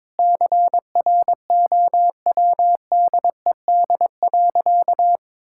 Morse code, heard as a keyed tone, about 22 wpm, 700 Hz. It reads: CROWDED.
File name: Sentences with 7-Word Phrases - 22wpm